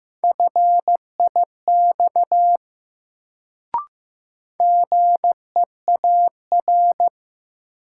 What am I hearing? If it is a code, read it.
Morse code, heard as a keyed tone, about 15 wpm, 700 Hz. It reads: FIX GEAR
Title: Sentences from Top 1400 Words - New Word Drill - 15wpm